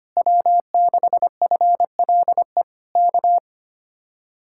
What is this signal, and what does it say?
Morse code, heard as a keyed tone, about 24 wpm, 700 Hz. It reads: W6FLE K
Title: Speed-Racing Common QSO Phrases Encore - 25wpm